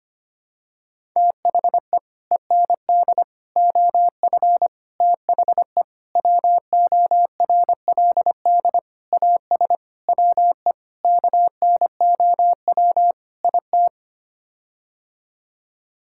Morse code, heard as a keyed tone, about 25 wpm, 700 Hz. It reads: THE END OF THE WORLD AS WE KNOW IT
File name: Sets of 9 Words - 25wpm